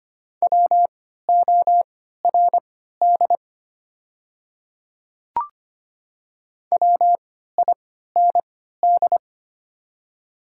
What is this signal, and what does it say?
Morse code, heard as a keyed tone, about 25 wpm, spaced as for 14 wpm, 700 Hz. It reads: WORD WIND
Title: 3x Character Spacing - Top 500 Words - 25wpm